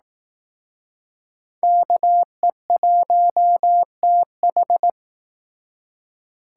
Morse code, heard as a keyed tone, about 18 wpm, 700 Hz. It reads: KE1TH
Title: Speed Racing - USA Call Signs 15wpm